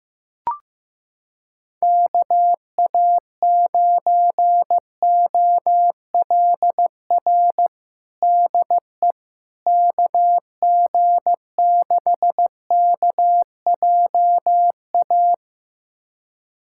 Morse code, heard as a keyed tone, about 15 wpm, 700 Hz. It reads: KA9OLR DE KG6KJA
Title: USA Callsign DE Callsign - 6 Characters - 15wpm